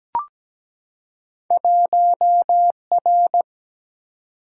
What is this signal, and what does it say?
Morse code, heard as a keyed tone, about 17 wpm, 700 Hz. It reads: 1R
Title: Single-Letter Single-Number 17wpm